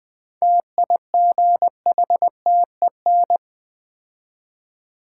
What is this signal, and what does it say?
Morse code, heard as a keyed tone, about 20 wpm, 700 Hz. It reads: TIGHTEN
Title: 7 Letter Words - First 3 Letters Given - 20wpm